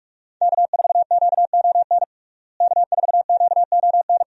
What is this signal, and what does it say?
Morse code, heard as a keyed tone, about 44 wpm, 700 Hz. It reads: K4QON K4QON
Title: USA Call Signs - 5 Characters - Repeated 2x - 45wpm